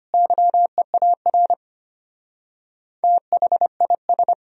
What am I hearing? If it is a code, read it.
Morse code, heard as a keyed tone, about 30 wpm, 700 Hz. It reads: YEAR THIS
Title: Speed-Racing - Sets of 2 Words 25wpm